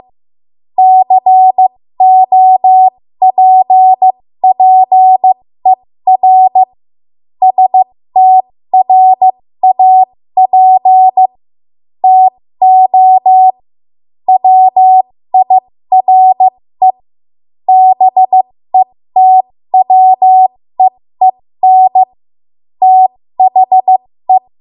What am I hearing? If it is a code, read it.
Morse code, heard as a keyed tone, about 15 wpm, spaced as for 13 wpm, 750 Hz. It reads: COPPER STRAP TO WIRE BETWEEN THE
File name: Morse / 13WPM